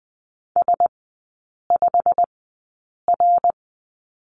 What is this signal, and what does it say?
Morse code, heard as a keyed tone, about 20 wpm, spaced as for 8 wpm, 700 Hz. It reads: S5R
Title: 087 Lesson 15e - Number 5 - Three Characters 20wpm